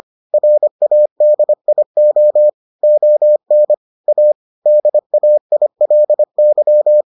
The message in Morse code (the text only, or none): RADIO ON A DAILY